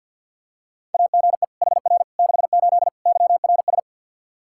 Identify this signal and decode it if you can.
Morse code, heard as a keyed tone, about 52 wpm, 700 Hz. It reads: AGE HR 68 YRS
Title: K1USN SST QSO Practice 50wpm